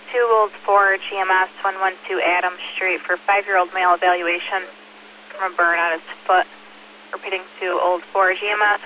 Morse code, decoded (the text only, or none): none